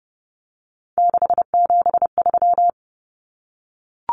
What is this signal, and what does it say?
Morse code, heard as a keyed tone, about 30 wpm, 700 Hz. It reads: 673
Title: Common Cut Numbers - 3-Digit 30wpm